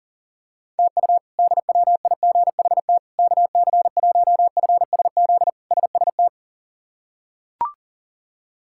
Morse code, heard as a keyed tone, about 40 wpm, 700 Hz. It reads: TU DWIGHT KC1FSZ SST
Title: K1USN SST QSO Practice - 40wpm